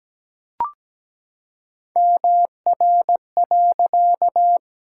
Morse code, heard as a keyed tone, about 17 wpm, 700 Hz. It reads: MR.
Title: Pangram Challenge v2 - 17wpm